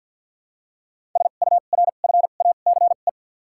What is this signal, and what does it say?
Morse code, heard as a keyed tone, about 53 wpm, 700 Hz. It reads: S U R F A C E
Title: 2x Character Spacing - Top 500 Words - 50wpm